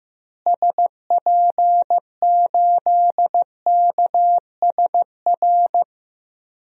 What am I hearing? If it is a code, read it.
Morse code, heard as a keyed tone, about 15 wpm, 700 Hz. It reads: SP8KSR
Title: DXCC-100 Call Signs - 6 Characters - Repeated 3x - 15wpm